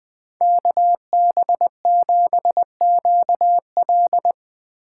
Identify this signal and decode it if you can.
Morse code, heard as a keyed tone, about 21 wpm, 700 Hz. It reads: KB7QL